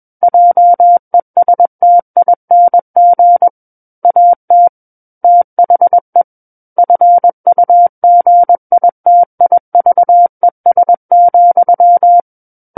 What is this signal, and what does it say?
Morse code, heard as a keyed tone, about 21 wpm, 700 Hz. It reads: JESTING AT THE FUGITIVES,
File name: Book One / 12-Chapter 12 - 21 WPM